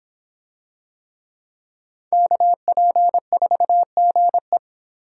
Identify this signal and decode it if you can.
Morse code, heard as a keyed tone, about 25 wpm, 700 Hz. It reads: KP4GE